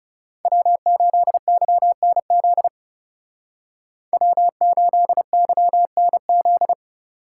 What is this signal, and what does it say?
Morse code, heard as a keyed tone, about 37 wpm, 700 Hz. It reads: W8YNZ W8YNZ
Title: Speed Racing - USA Call Signs 30wpm